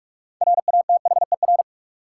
Morse code, heard as a keyed tone, about 44 wpm, 700 Hz. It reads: RATHER
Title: Sets of 2 Words - New Word Drill - 45wpm